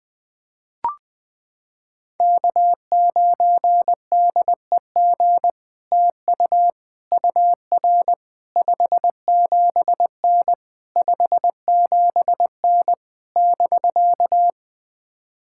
Morse code, heard as a keyed tone, about 20 wpm, 700 Hz. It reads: K9DEG TU UR 57N 57N <BK>
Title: POTA QSO Practice - 20wpm